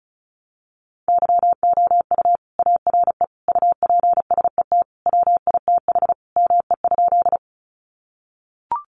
Morse code, heard as a keyed tone, about 34 wpm, 700 Hz. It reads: YOU ARE UPSET WITH ME?
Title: Sentences from Top 800 Words 35wpm